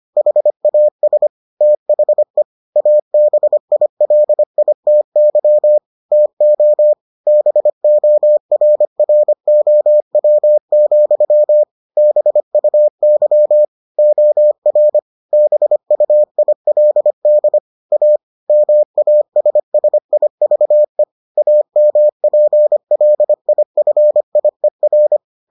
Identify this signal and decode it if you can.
Morse code, heard as a keyed tone, about 25 wpm, 600 Hz. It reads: HAS THE ABILITY TO BORROW, BUY OR BUILD A MASSIVE AMPLIFIER